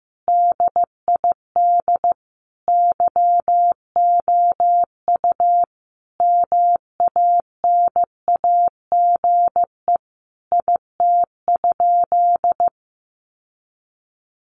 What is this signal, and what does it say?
Morse code, heard as a keyed tone, about 15 wpm, 700 Hz. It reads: DID YOU MANAGE IT?
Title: Sentences from Top 1400 Words 15wpm